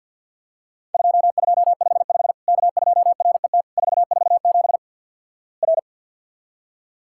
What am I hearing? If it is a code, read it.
Morse code, heard as a keyed tone, about 49 wpm, 700 Hz. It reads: 2255 K2RET 447 R
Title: Worked All Europe DX Contest - QTC Exchange - 50wpm